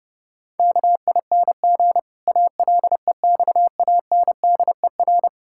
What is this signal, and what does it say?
Morse code, heard as a keyed tone, about 29 wpm, 700 Hz. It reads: KING ALEXANDER
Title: Pangram Challenge v2 - 30wpm